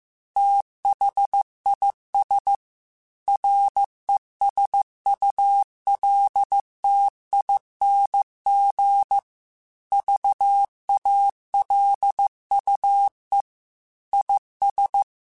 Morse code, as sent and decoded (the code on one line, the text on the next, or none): - .... .. ... / .-. ... ..- .-.. - .. -. --. / ...- .- .-.. ..- . / .. ...
THIS RESULTING VALUE IS